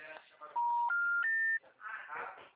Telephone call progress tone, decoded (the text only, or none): special information tone